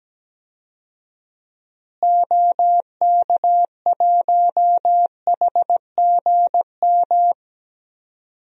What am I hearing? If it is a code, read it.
Morse code, heard as a keyed tone, about 17 wpm, 700 Hz. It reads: OK1HGM
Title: DXCC-100 Call Signs - 6 Characters - Repeated 2x - 17wpm